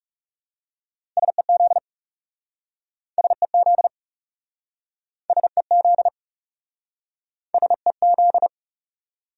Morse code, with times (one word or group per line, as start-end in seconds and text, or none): 1.2-1.8 SEZ
3.2-3.9 SEZ
5.3-6.1 SEZ
7.5-8.5 SEZ